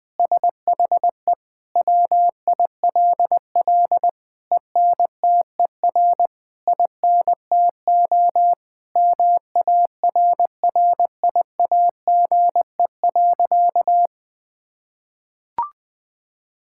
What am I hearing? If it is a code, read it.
Morse code, heard as a keyed tone, about 20 wpm, 700 Hz. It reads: SHE WILL ENTER INTO MARRIAGE.
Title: Sentences from Top 1000 Words - Review Part 6 - 20wpm